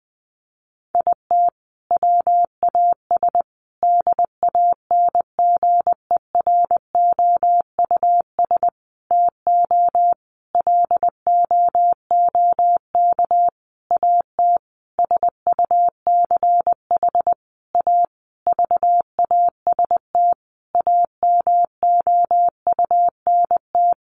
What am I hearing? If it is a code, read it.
Morse code, heard as a keyed tone, about 19 wpm, 700 Hz. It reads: IT WAS DANGEROUS TO LOOK AT SUCH A VAST AMOUNT
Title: Sentences from Top 1300 Words 20wpm